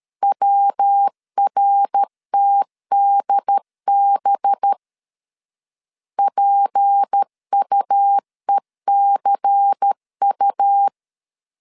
Morse code, heard as a keyed tone, about 13 wpm, 800 Hz. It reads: WRTDB PUECU
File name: LEZIONI / 6 PKX0 / 6 P 12wpm 2033